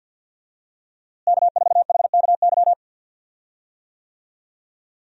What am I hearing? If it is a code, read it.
Morse code, heard as a keyed tone, about 51 wpm, 700 Hz. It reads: K4HKY
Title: USA Call Signs - 5 Characters - Repeated 2x - 50wpm